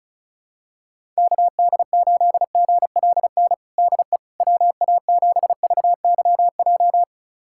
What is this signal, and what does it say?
Morse code, heard as a keyed tone, about 36 wpm, 700 Hz. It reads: KD8GLN DE WA7VYJ
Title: USA Callsign DE Callsign 35wpm